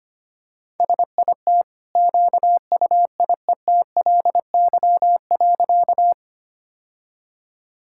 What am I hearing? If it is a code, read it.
Morse code, heard as a keyed tone, about 25 wpm, 700 Hz. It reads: SIT QUIETLY.
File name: Easy Sentences with Hints - 25wpm